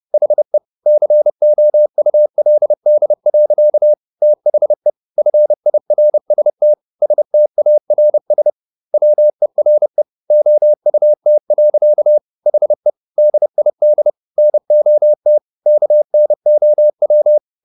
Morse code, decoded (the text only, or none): HE COULD. THE FIRST STARS WERE OUT. HE DID NOT KNOW